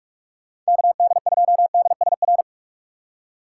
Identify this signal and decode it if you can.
Morse code, heard as a keyed tone, about 46 wpm, 700 Hz. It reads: KD2DSR
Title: USA Call Signs - 6 Characters - 2x Character Spacing - 45wpm